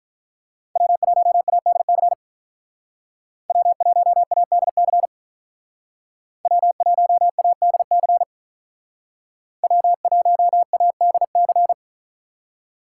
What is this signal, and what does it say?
Morse code, heard as a keyed tone, about 46 wpm, 700 Hz. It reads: W1ADC W1ADC W1ADC W1ADC